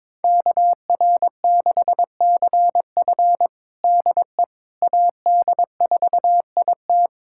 Morse code, heard as a keyed tone, about 22 wpm, 700 Hz. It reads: KR6CF DE AD4IT
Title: USA Callsign DE Callsign - 5 Characters - 22wpm